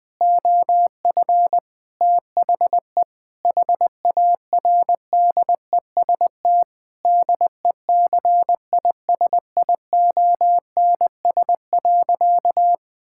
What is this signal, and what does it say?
Morse code, heard as a keyed tone, about 21 wpm, 700 Hz. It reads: OF THE HARDEST DECISIONS.
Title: Sentences with 5-Word Phrases - Spoken Beginning - 20wpm